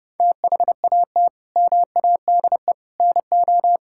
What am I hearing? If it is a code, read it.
Morse code, heard as a keyed tone, about 32 wpm, 700 Hz. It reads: THAT MADE NO
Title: Sentences from Top 1000 Words - Review Part 5 - 30wpm